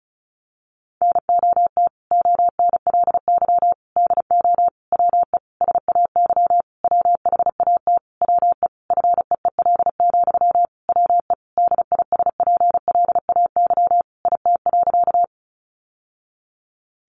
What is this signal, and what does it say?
Morse code, heard as a keyed tone, about 34 wpm, 700 Hz. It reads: NOT ONLY DO WE SAY WHAT WE FEEL, WE DISPLAY IT.
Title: Sentences from Top 1200 Words 35wpm